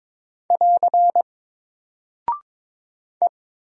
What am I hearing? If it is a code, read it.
Morse code, heard as a keyed tone, about 22 wpm, 700 Hz. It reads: <AR> E
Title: Single Character (Letter:Number:Punctuation:Prosign) 22wpm